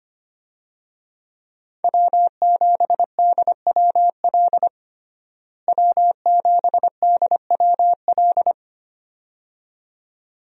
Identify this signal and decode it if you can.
Morse code, heard as a keyed tone, about 25 wpm, 700 Hz. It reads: W7DWL W7DWL
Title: USA Call Signs - 5 Characters - Repeated 2x - 25wpm